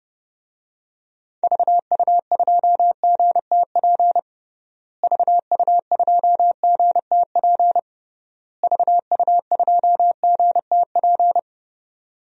Morse code, heard as a keyed tone, about 30 wpm, 700 Hz. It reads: VU2GTP VU2GTP VU2GTP
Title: DXCC-100 Call Signs - 6 Characters - Repeated 3x - 30wpm